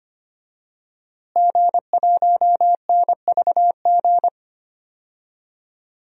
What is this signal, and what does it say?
Morse code, heard as a keyed tone, about 25 wpm, 700 Hz. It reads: G1NVG
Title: DXCC-100 Call Signs - 5 Characters - Repeated 2x - 25wpm